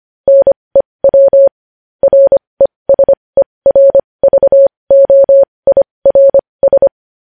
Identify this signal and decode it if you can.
Morse code, heard as a keyed tone, about 25 wpm, spaced as for 20 wpm, 550 Hz. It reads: NEW RESERVOIRS
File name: podcast / News headlines-1x-25x20